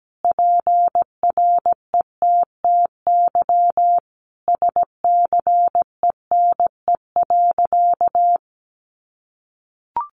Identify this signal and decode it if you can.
Morse code, heard as a keyed tone, about 17 wpm, 700 Hz. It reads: PRETTY SCENE.
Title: Sentences from Top 1100 Words 17wpm